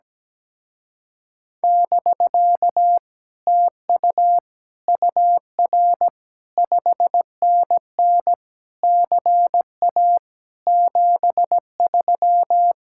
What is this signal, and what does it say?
Morse code, heard as a keyed tone, about 17 wpm, 700 Hz. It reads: <BK> TU UR 5NN CA 73
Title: Speed-Racing Common QSO Phrases Encore - 17wpm